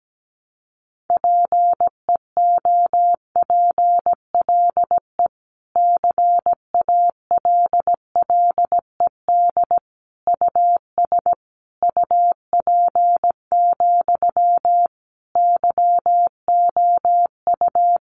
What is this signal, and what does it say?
Morse code, heard as a keyed tone, about 17 wpm, 700 Hz. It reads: PEOPLE CALLED US UP, YOU
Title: A Sound Of Thunder 17wpm